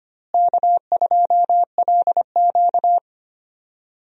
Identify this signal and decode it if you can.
Morse code, heard as a keyed tone, about 25 wpm, 700 Hz. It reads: K2LQ
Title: USA Call Signs - 4 Characters - Repeated 3x - 25wpm